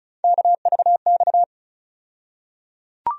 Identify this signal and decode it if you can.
Morse code, heard as a keyed tone, about 36 wpm, 700 Hz. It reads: KVX
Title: USA Call Sign Postfixes - 3 Letters 35wpm